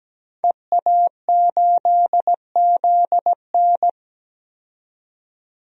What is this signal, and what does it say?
Morse code, heard as a keyed tone, about 17 wpm, 700 Hz. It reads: EA8ZN
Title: DXCC-100 Call Signs - 5 Characters - 17wpm